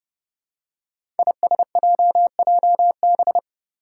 Morse code, heard as a keyed tone, about 32 wpm, 700 Hz. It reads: ISJJB
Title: Five Challenging Letter-Number Combinations 30wpm